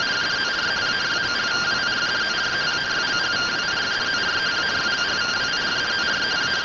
Two-tone teleprinter signal, about 170 Hz shift.